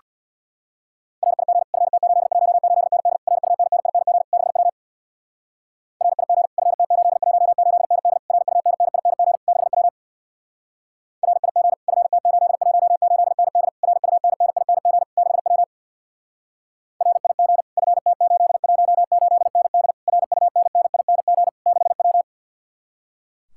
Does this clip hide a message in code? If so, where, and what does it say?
Yes: Morse code from 1.2 to 22.2 s, heard as a keyed tone, about 66 wpm, 700 Hz. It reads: RIG FT818ND RUNNING 6W RIG FT818ND RUNNING 6W RIG FT818ND RUNNING 6W RIG FT818ND RUNNING 6W